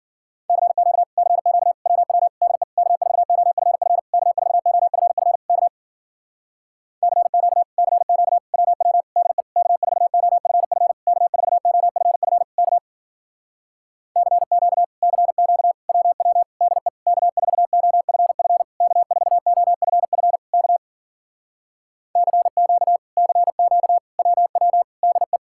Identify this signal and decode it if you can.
Morse code, heard as a keyed tone, about 53 wpm, 700 Hz. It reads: CQ CQ WW DE K4OFF K4OFF K CQ CQ WW DE K4OFF K4OFF K CQ CQ WW DE K4OFF K4OFF K CQ CQ WW DE